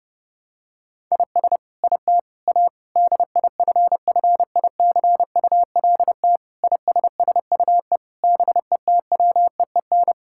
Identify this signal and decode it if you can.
Morse code, heard as a keyed tone, about 30 wpm, 700 Hz. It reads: IS IT A DIFFICULT ISSUE BETWEEN
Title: Sentences from Top 1000 Words - Review Part 6 - 30wpm